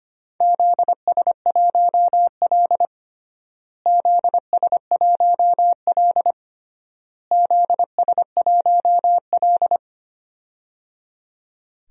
Morse code, heard as a keyed tone, about 25 wpm, 700 Hz. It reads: ZS1L ZS1L ZS1L
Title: DXCC-100 Call Signs - 4 Characters - Repeated 3x - 25wpm